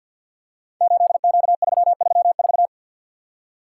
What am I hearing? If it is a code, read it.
Morse code, heard as a keyed tone, about 50 wpm, 700 Hz. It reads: 8Q334
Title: Code Groups - 5 Letter:Numbers - 50wpm